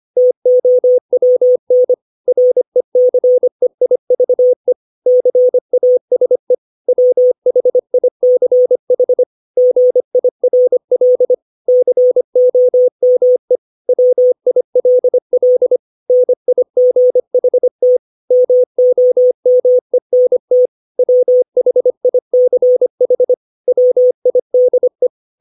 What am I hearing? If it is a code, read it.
Morse code, heard as a keyed tone, about 25 wpm, 500 Hz. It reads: TOWN RECEIVE CASE WHICH GIRL COME WILL NIGHT MOMENT WHICH WIDE